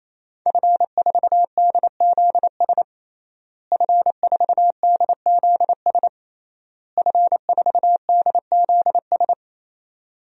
Morse code, heard as a keyed tone, about 28 wpm, 700 Hz. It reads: F4DZS F4DZS F4DZS
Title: DXCC-100 Call Signs - 5 Characters - Repeated 3x - 28wpm